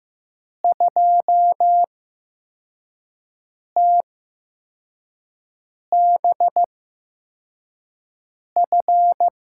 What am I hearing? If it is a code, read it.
Morse code, heard as a keyed tone, about 15 wpm, spaced as for 4 wpm, 700 Hz. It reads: 2TBF
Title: USA Call Signs - 6 Characters - 8x Character Spacing - 15wpm